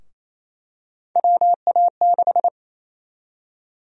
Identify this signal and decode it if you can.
Morse code, heard as a keyed tone, about 28 wpm, 700 Hz. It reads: WA6
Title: USA Call Sign Prefixes - 28wpm